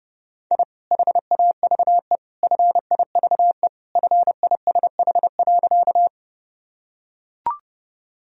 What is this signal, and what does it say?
Morse code, heard as a keyed tone, about 30 wpm, 700 Hz. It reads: I HAVE FIVE FISH.
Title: Sentences from Top 1000 Words - Review Part 2 - 30wpm